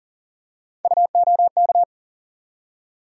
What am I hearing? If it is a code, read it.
Morse code, heard as a keyed tone, about 40 wpm, 700 Hz. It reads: UOK
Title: USA Call Sign Postfixes - 3 Letters 40wpm